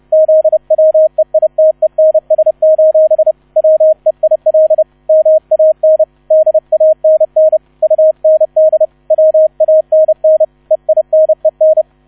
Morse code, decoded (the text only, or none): ZWEITENS: WEIL MAN DANN UND WANN EINEN